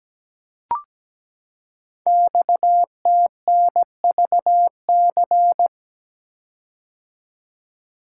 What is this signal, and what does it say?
Morse code, heard as a keyed tone, about 17 wpm, 700 Hz. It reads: XTNVC
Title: Code Groups - 5 Letter:Numbers - 17wpm